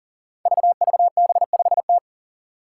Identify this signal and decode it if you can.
Morse code, heard as a keyed tone, about 40 wpm, 700 Hz. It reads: VVB5T
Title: Five Challenging Letter-Number Combinations 40wpm